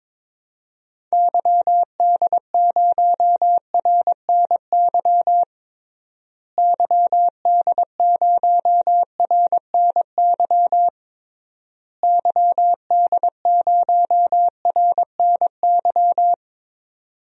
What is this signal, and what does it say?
Morse code, heard as a keyed tone, about 22 wpm, 700 Hz. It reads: YD0RNY YD0RNY YD0RNY
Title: DXCC-100 Call Signs - 6 Characters - Repeated 3x - 22wpm